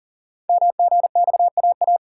Morse code, heard as a keyed tone, about 40 wpm, 690 Hz. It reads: MGXAA
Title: Code Groups - 5 Letters - 40wpm